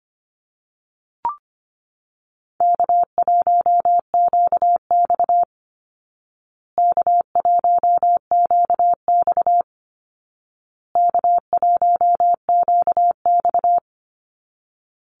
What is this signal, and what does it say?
Morse code, heard as a keyed tone, about 25 wpm, 700 Hz. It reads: K1QX K1QX K1QX
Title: Call Signs Repeated 2x - 25wpm